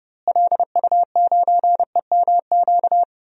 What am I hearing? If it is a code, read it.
Morse code, heard as a keyed tone, about 30 wpm, 700 Hz. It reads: LU9EMQ
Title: DXCC-100 Call Signs - 6 Characters - Repeated 2x - 30wpm